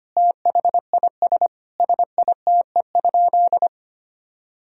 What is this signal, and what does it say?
Morse code, heard as a keyed tone, about 25 wpm, 700 Hz. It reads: THIS SITE?
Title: Sentences from Top 1000 Words - Review Part 4 - 25wpm